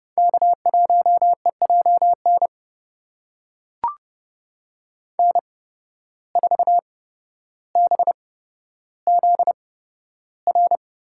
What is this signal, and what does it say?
Morse code, heard as a keyed tone, about 30 wpm, spaced as for 13 wpm, 700 Hz. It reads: K1EJN N4BZR
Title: USA Call Signs - 5 Characters - 8x Character Spacing - 30wpm